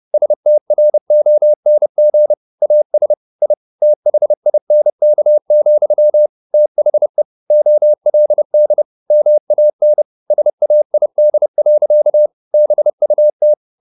Morse code, heard as a keyed tone, about 30 wpm, 600 Hz. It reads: STRONG AS I THINK, THE OLD MAN SAID. BUT
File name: the old man and the sea / the old man and the sea 1 30wpm